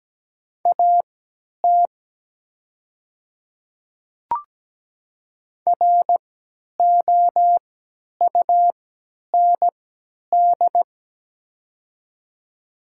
Morse code, heard as a keyed tone, about 17 wpm, spaced as for 10 wpm, 700 Hz. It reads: AT ROUND
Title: 3x Character Spacing - Top 300 Words - 17wpm